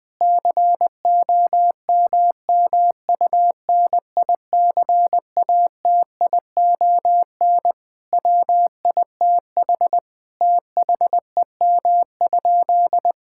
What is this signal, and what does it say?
Morse code, heard as a keyed tone, about 19 wpm, 700 Hz. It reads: COMMUNICATION WITH THEM?
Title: Sentences from Top 1000 Words - Review Part 6 - 20wpm